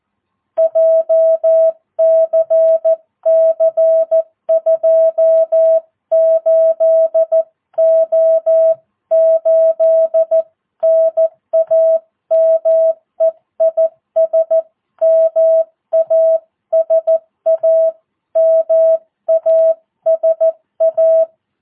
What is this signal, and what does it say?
Morse code, heard as a keyed tone, about 14 wpm, 650 Hz. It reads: JCC 28O8 NAME IS MASA MASA
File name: jh8cbh / 2022cw